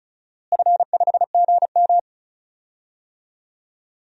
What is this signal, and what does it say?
Morse code, heard as a keyed tone, about 36 wpm, 700 Hz. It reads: F5GM
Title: DXCC-100 Call Signs - 4 Characters - Repeated 2x - 35wpm